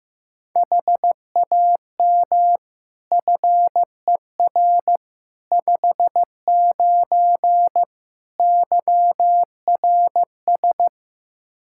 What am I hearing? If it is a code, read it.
Morse code, heard as a keyed tone, about 15 wpm, 700 Hz. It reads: HAM FER 59 YRS
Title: Practice QSO - Exchange 3 - More Info - 15wpm